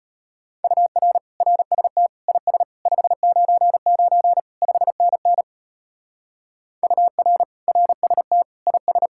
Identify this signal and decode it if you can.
Morse code, heard as a keyed tone, about 32 wpm, 700 Hz. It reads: UR RST IS 599 5NN UR RST IS